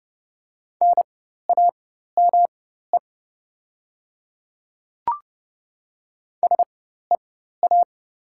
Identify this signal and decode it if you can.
Morse code, heard as a keyed tone, about 29 wpm, spaced as for 14 wpm, 690 Hz. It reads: NAME SEA
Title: 4x Character Spacing - Top 300 Words - 30wpm